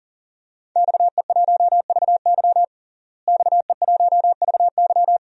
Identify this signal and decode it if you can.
Morse code, heard as a keyed tone, about 40 wpm, 700 Hz. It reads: XE1VY XE1VY
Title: DXCC-100 Call Signs - 5 Characters - Repeated 2x - 40wpm